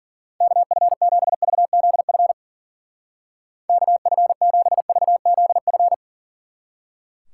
Morse code, heard as a keyed tone, about 51 wpm, 700 Hz. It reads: KF7VZF KF7VZF